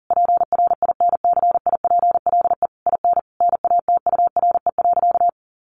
Morse code, heard as a keyed tone, about 40 wpm, 700 Hz. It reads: PRINCIPLE IN NATURE.
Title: Sentences from Top 1400 Words 40wpm